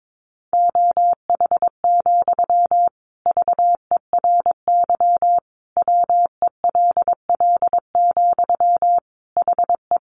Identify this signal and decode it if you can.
Morse code, heard as a keyed tone, about 22 wpm, 700 Hz. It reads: OH, VERY WELL, HE
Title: Sentences from Top 500 Words 22wpm